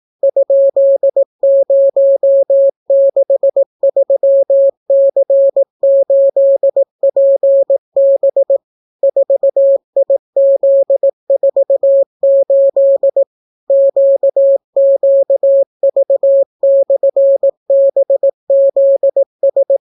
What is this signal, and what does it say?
Morse code, heard as a keyed tone, about 18 wpm, 550 Hz. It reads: ?063C8PB 4IZ48 QQV/BZS